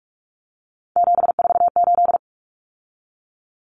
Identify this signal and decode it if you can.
Morse code, heard as a keyed tone, about 44 wpm, 700 Hz. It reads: 748
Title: Common Cut Numbers - 3-Digit 45wpm